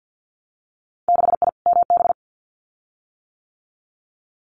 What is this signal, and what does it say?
Morse code, heard as a keyed tone, about 50 wpm, 700 Hz. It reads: 6I MB
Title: Winter Field Day Exchanges Encore 50wpm